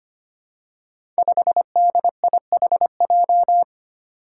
Morse code, heard as a keyed tone, about 25 wpm, 700 Hz. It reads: 5DIHJ
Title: Five Challenging Letter-Number Combinations 25wpm